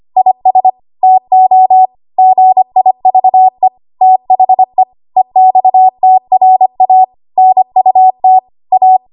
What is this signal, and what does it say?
Morse code, heard as a keyed tone, about 25 wpm, 750 Hz. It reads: IS TO GIVE THE EXTRA NUT A
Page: 25 WPM Code Archive